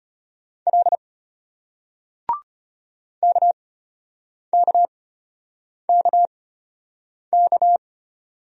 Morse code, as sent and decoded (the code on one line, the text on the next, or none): .-.. / -.- -.- -.- -.-
L KKKK